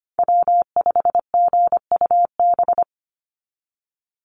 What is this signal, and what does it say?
Morse code, heard as a keyed tone, about 25 wpm, 700 Hz. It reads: W5GUB
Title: USA Call Signs 25wpm